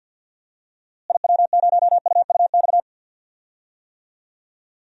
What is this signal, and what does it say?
Morse code, heard as a keyed tone, about 53 wpm, 700 Hz. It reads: IW0UUX
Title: DXCC-100 Call Signs - 6 Characters - Repeated 2x - 50wpm